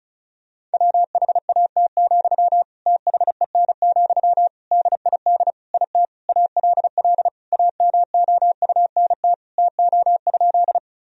Morse code, heard as a keyed tone, about 36 wpm, 700 Hz. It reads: WHAT, THEN, DID IT ALL AMOUNT TO?
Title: Sentences from Top 1000 Words - Review Part 4 - 35wpm